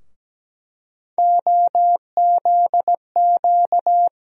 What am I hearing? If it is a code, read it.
Morse code, heard as a keyed tone, about 17 wpm, 700 Hz. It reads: OZQ
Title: USA Call Sign Postfixes - 3 Letters 17wpm